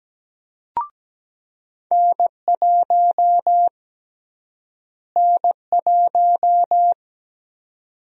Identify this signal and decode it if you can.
Morse code, heard as a keyed tone, about 17 wpm, 700 Hz. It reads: N1 N1
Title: USA Call Sign Prefixes - Repeated 2x - 17wpm